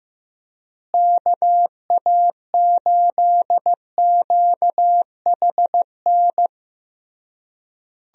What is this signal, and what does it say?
Morse code, heard as a keyed tone, about 15 wpm, 700 Hz. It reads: KA8QHN
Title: USA Call Signs - 6 Characters - Repeated 3x - 15wpm